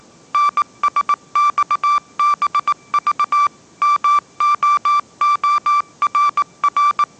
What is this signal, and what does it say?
Morse code, heard as a keyed tone, about 19 wpm, 1170 Hz. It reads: NSXBV MOORR